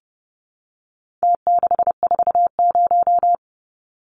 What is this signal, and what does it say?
Morse code, heard as a keyed tone, about 30 wpm, 700 Hz. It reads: T640
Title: Numbers - Contest Style - 4-Digit 30wpm